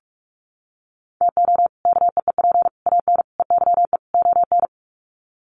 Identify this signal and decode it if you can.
Morse code, heard as a keyed tone, about 44 wpm, 700 Hz. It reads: TO KEEP AN EYE ON
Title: Sets of 5 Words Encore 45wpm